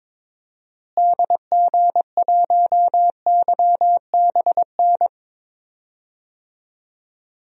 Morse code, heard as a keyed tone, about 22 wpm, 700 Hz. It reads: DG1YBN
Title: DXCC-100 Call Signs - 6 Characters - Repeated 2x - 22wpm